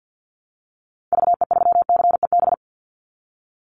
Morse code, heard as a keyed tone, about 50 wpm, 700 Hz. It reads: VE3CEB
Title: Call Signs - 50wpm